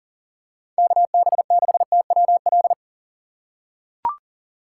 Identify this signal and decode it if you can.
Morse code, heard as a keyed tone, about 37 wpm, 700 Hz. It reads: KB6TWL
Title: USA Call Signs - 6 Characters - 4x Character Spacing - 40wpm